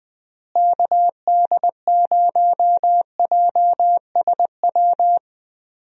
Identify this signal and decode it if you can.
Morse code, heard as a keyed tone, about 21 wpm, 700 Hz. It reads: KD0JSW
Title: USA Call Signs - 6 Characters - Repeated 3x - 20wpm